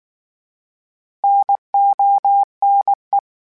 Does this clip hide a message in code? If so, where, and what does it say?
Yes: Morse code from 1.2 to 3.2 s, heard as a keyed tone, about 19 wpm, 800 Hz. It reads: NONE